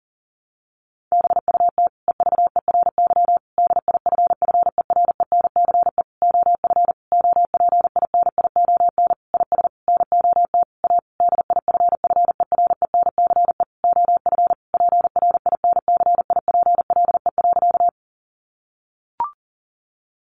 Morse code, heard as a keyed tone, about 40 wpm, 700 Hz. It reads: BUT EVERY DIFFERENCE OF OPINION IS NOT A DIFFERENCE OF PRINCIPLE.
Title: Sentences from Top 1400 Words 40wpm